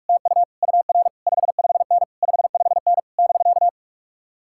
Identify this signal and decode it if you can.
Morse code, heard as a keyed tone, about 56 wpm, 700 Hz. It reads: TU UR 55N 55N <BK>
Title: Speed-Racing Common QSO Phrases - 45wpm